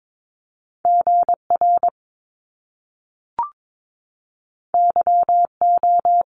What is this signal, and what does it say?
Morse code, heard as a keyed tone, about 22 wpm, 700 Hz. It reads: GR YO
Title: Top Two Letter Combinations in Words 22wpm